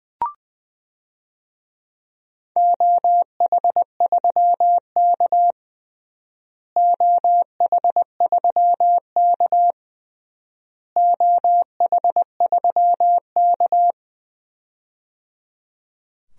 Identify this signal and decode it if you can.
Morse code, heard as a keyed tone, about 20 wpm, 700 Hz. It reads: OH3K OH3K OH3K
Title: DXCC-100 Call Signs - 4 Characters - Repeated 3x - 20wpm